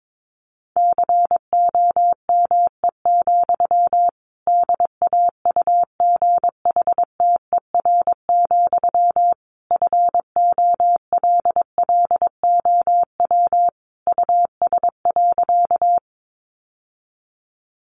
Morse code, heard as a keyed tone, about 22 wpm, 700 Hz. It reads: COME, DAUGHTER, FOLLOW US.
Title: Sentences from Top 1500 Words 22wpm